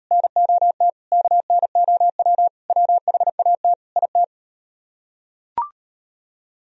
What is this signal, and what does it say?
Morse code, heard as a keyed tone, about 38 wpm, 700 Hz. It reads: NOT KNOW WHAT IT